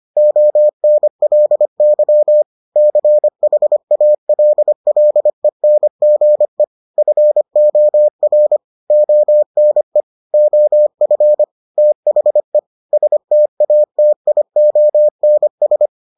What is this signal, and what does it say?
Morse code, heard as a keyed tone, about 25 wpm, 600 Hz. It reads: ONLY CHALLENGE FOR ONE OF THE STATIONS